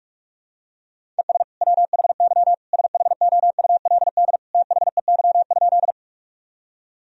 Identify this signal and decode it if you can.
Morse code, heard as a keyed tone, about 41 wpm, 700 Hz. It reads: ES WHY SHOULD THEY?
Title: Sentences from Top 300 Words and Abbreviations 45wpm